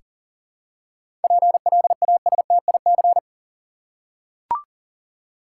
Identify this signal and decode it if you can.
Morse code, heard as a keyed tone, about 40 wpm, 700 Hz. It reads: PLASTIC
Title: Seven Letter Words - 40wpm